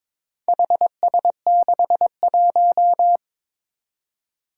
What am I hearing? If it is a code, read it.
Morse code, heard as a keyed tone, about 21 wpm, 700 Hz. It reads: HS61
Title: Four Challenging Letter-Number Combinations 22wpm